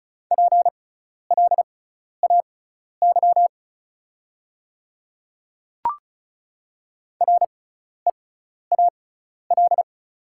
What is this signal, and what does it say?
Morse code, heard as a keyed tone, about 36 wpm, spaced as for 12 wpm, 690 Hz. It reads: PLAY REAL
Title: 6x Character Spacing - Top 300 Words - 35wpm